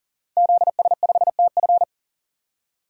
Morse code, heard as a keyed tone, about 40 wpm, 700 Hz. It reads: ZS5TF
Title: DXCC-100 Call Signs - 5 Characters - 40wpm